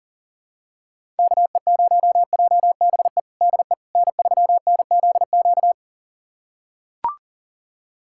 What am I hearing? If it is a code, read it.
Morse code, heard as a keyed tone, about 37 wpm, 700 Hz. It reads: KE0JBE DE N3NZQ
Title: USA Callsign DE Callsign 40wpm